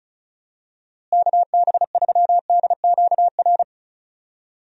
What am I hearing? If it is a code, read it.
Morse code, heard as a keyed tone, about 34 wpm, 700 Hz. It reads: KB3DQR
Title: USA Call Signs - 6 Characters - Repeated 3x - 35wpm